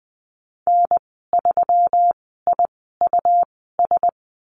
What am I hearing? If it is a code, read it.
Morse code, heard as a keyed tone, about 20 wpm, spaced as for 15 wpm, 700 Hz. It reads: N3IUS
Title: 154 Lesson 25g - Letter p - Call Signs 20wpm